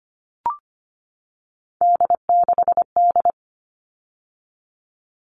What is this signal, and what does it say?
Morse code, heard as a keyed tone, about 25 wpm, 700 Hz. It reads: D6D
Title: All Cut Numbers - 3-Digit 25wpm